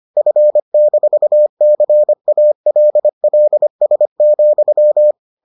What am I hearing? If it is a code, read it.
Morse code, heard as a keyed tone, about 25 wpm, 600 Hz. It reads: F-CALLS,